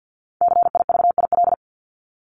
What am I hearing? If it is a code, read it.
Morse code, heard as a keyed tone, about 50 wpm, 700 Hz. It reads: CIVIL
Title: Top 1751-2000 Words 50wpm